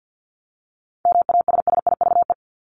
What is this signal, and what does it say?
Morse code, heard as a keyed tone, about 50 wpm, 700 Hz. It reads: MASSIVE
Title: Top 1751-2000 Words 50wpm